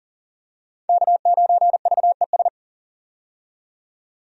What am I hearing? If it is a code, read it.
Morse code, heard as a keyed tone, about 40 wpm, 700 Hz. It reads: K9VES
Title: USA Call Signs - 5 Characters - 40wpm